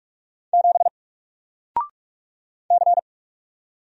Morse code, heard as a keyed tone, about 44 wpm, 700 Hz. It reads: 7 C
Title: Single Character (Letter:Number:Punctuation:Prosign) Alt 45wpm